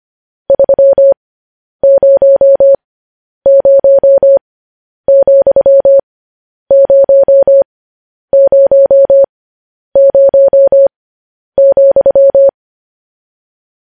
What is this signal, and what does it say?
Morse code, heard as a keyed tone, about 25 wpm, spaced as for 10 wpm, 550 Hz. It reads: 300,000,